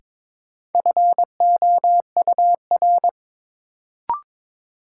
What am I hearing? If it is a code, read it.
Morse code, heard as a keyed tone, about 22 wpm, 700 Hz. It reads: FOUR
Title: Sentences from Top 300 Words - New Word Drill - 22wpm